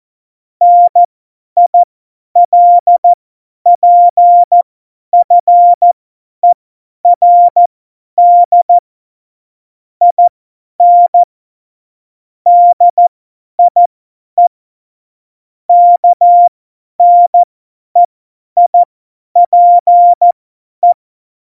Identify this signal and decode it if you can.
Morse code, heard as a keyed tone, about 14 wpm, spaced as for 10 wpm, 700 Hz. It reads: NILPFERD IN DIE KNEIPE
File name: Morsekurs / Witze / Witz 29. Dezember 2020, 50 F70